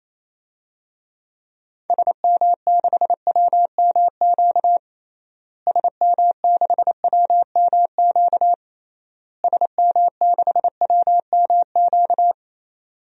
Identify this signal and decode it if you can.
Morse code, heard as a keyed tone, about 28 wpm, 700 Hz. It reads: SM6WMQ SM6WMQ SM6WMQ
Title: DXCC-100 Call Signs - 6 Characters - Repeated 3x - 28wpm